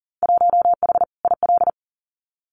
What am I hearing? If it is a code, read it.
Morse code, heard as a keyed tone, about 40 wpm, 700 Hz. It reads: 1H IL
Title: Winter Field Day Exchanges 40wpm